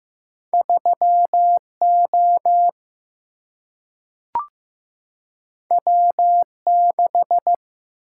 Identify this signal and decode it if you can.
Morse code, heard as a keyed tone, about 15 wpm, 700 Hz. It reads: 3 O W 6
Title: Single-Letter Single-Number 15wpm